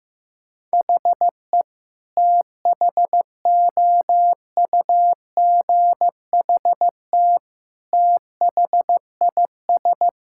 Morse code, heard as a keyed tone, about 16 wpm, 700 Hz. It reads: HE THOUGHT THIS
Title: Sentences from Top 1600 Words 15wpm